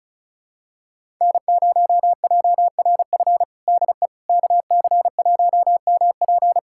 Morse code, heard as a keyed tone, about 36 wpm, 700 Hz. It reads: N0JRF DE KC1MP
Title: USA Callsign DE Callsign - 5 Characters - 35wpm